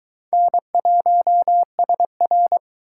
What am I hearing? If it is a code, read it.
Morse code, heard as a keyed tone, about 23 wpm, 700 Hz. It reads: N1SR